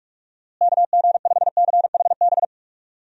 Morse code, heard as a keyed tone, about 44 wpm, 700 Hz. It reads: KG5CHB
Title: USA Call Signs - 6 Characters - Repeated 2x - 45wpm